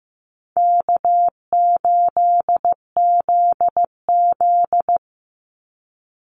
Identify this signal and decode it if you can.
Morse code, heard as a keyed tone, about 15 wpm, 700 Hz. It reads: K8ZZ
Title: Easy Call Signs Repeated 3x - 15wpm